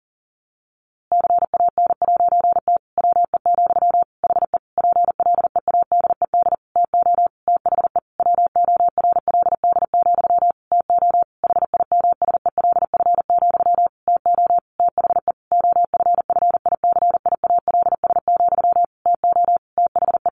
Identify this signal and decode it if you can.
Morse code, heard as a keyed tone, about 40 wpm, 700 Hz. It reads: CAN'T WE, HE PLEADED TO THE WORLD, TO HIMSELF, TO THE OFFICIALS, TO THE
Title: A Sound Of Thunder 40wpm